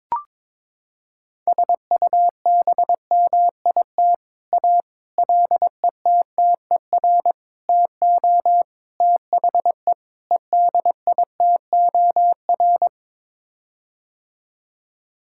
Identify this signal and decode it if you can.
Morse code, heard as a keyed tone, about 22 wpm, 700 Hz. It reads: SUBMIT A LETTER TO THE EDITOR
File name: practice / 22wpm / Sets of 6 Words - 22wpm